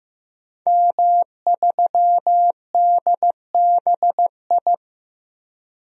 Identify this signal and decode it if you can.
Morse code, heard as a keyed tone, about 15 wpm, 700 Hz. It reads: M3DBI
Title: DXCC-100 Call Signs - 5 Characters - Repeated 2x - 15wpm